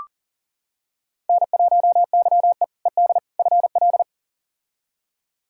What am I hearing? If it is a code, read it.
Morse code, heard as a keyed tone, about 40 wpm, 700 Hz. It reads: N1YE ED FL
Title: K1USN SST QSO Practice - 40wpm